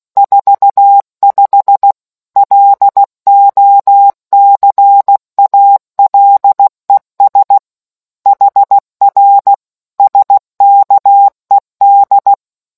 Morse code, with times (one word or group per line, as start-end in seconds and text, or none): 0.2-1.9 45
2.4-7.6 LOCALES
8.3-9.5 HR
10.0-12.3 SKED